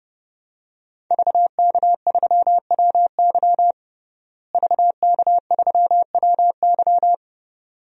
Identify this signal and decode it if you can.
Morse code, heard as a keyed tone, about 30 wpm, 700 Hz. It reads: VK3WY VK3WY
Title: DXCC-100 Call Signs - 5 Characters - Repeated 2x - 30wpm